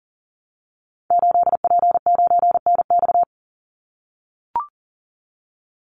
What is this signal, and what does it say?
Morse code, heard as a keyed tone, about 40 wpm, 700 Hz. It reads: 8P9NX
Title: Call Signs - 40wpm